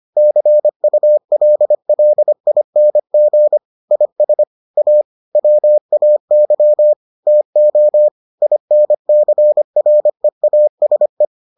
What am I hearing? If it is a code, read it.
Morse code, heard as a keyed tone, about 25 wpm, 600 Hz. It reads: CULLING IS A WAY TO INCREASE